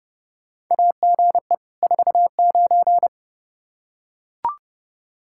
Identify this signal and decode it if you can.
Morse code, heard as a keyed tone, about 30 wpm, 700 Hz. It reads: AGE 49
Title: K1USN SST QSO Practice 30wpm